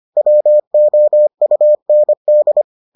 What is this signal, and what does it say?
Morse code, heard as a keyed tone, about 25 wpm, 600 Hz. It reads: WOUND